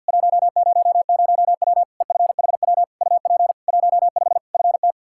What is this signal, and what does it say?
Morse code, heard as a keyed tone, about 48 wpm, 700 Hz. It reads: 100W EFHW UP 15 FT
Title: Practice QSO - Exchange 2 - Info - 50wpm